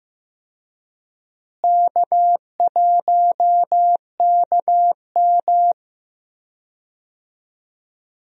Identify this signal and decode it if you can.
Morse code, heard as a keyed tone, about 15 wpm, 700 Hz. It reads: K1KM
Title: USA Call Signs - 4 Characters - Repeated 2x - 15wpm